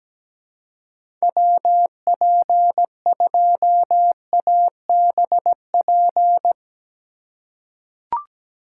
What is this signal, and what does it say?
Morse code, heard as a keyed tone, about 17 wpm, 700 Hz. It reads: WP2ABP
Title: USA Call Signs - 3x Character Spacing - 17wpm